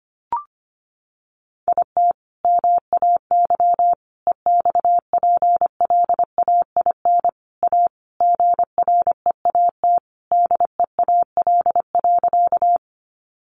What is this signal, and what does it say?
Morse code, heard as a keyed tone, about 25 wpm, 700 Hz. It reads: IT MAY EXPLAIN A GREAT DEAL.
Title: Sentences from Top 1100 Words 25wpm